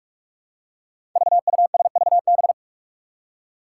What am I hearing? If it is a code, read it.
Morse code, heard as a keyed tone, about 41 wpm, 700 Hz. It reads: VUSVB
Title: Five Challenging Letter-Number Combinations 45wpm